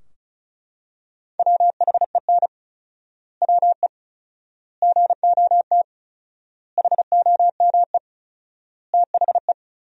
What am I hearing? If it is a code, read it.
Morse code, heard as a keyed tone, about 36 wpm, 700 Hz. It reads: WHEN WE GOT HOME THE
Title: 4x Word-Spacing - Sentences from Top 500 Words - 35wpm